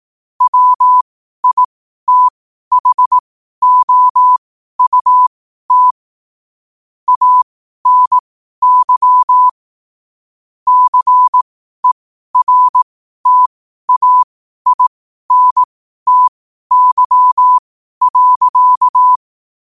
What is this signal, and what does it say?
Morse code, heard as a keyed tone, about 18 wpm, spaced as for 12 wpm, 1000 Hz. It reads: WITHOUT ANY CERTAINTY.
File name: qotdfiles / 12wpm